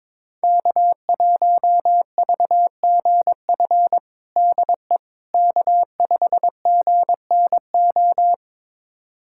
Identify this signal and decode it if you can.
Morse code, heard as a keyed tone, about 21 wpm, 700 Hz. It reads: K1VGF DE K5GNO
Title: USA Callsign DE Callsign - 5 Characters - 22wpm